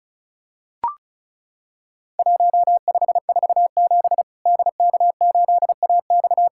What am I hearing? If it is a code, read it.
Morse code, heard as a keyed tone, about 36 wpm, 700 Hz. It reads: 1547 DK8AX
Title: Worked All Europe DX Contest - QTC Exchange - 35wpm